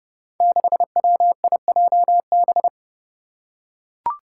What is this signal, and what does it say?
Morse code, heard as a keyed tone, about 32 wpm, 700 Hz. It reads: 6WIJB
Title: Five Challenging Letter-Number Combinations 30wpm